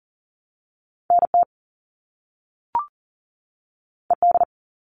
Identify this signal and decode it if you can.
Morse code, heard as a keyed tone, about 42 wpm, 700 Hz. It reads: NT ED
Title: All Cut Numbers - 2-Digit 40wpm